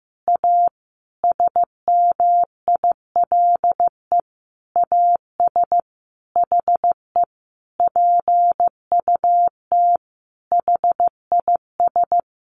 Morse code, heard as a keyed tone, about 15 wpm, 700 Hz. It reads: A SMILE AS HE PUT HIS
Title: A Sound Of Thunder 15wpm